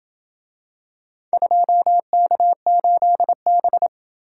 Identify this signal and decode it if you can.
Morse code, heard as a keyed tone, about 28 wpm, 700 Hz. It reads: 2K8B